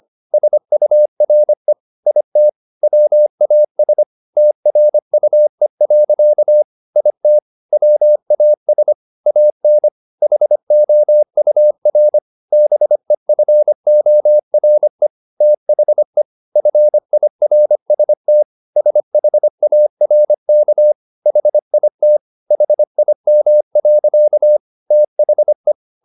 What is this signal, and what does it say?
Morse code, heard as a keyed tone, about 25 wpm, 600 Hz. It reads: SURE IT WAS TRUE. IT WAS AN HOUR BEFORE THE FIRST SHARK HIT HIM. THE